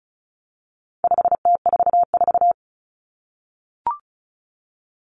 Morse code, heard as a keyed tone, about 34 wpm, 700 Hz. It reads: 5T44
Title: Common Cut Numbers - 4-Digit 35wpm